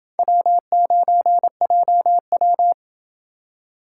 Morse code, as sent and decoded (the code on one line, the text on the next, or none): .-- ----. .--- .--
W9JW